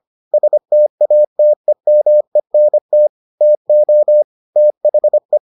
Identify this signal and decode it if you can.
Morse code, heard as a keyed tone, about 25 wpm, 600 Hz. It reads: STATEMENT TO THE